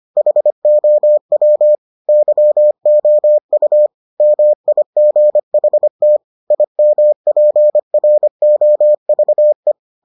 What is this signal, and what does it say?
Morse code, heard as a keyed tone, about 25 wpm, 600 Hz. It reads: HOW YOU MIGHT IMPROVE